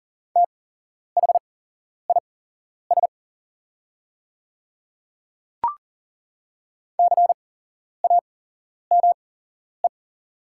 Morse code, heard as a keyed tone, about 39 wpm, spaced as for 11 wpm, 690 Hz. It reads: THIS CAME
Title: 8x Character Spacing - Top 500 Words - 40wpm